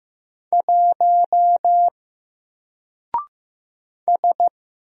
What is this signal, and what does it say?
Morse code, heard as a keyed tone, about 15 wpm, 700 Hz. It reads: 1 S
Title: Single Character (Letter:Number:Punctuation:Prosign) 15wpm